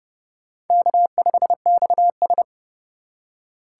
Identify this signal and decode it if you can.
Morse code, heard as a keyed tone, about 31 wpm, 700 Hz. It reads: K5XS
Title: USA Call Signs - 4 Characters - 2x Character Spacing - 30wpm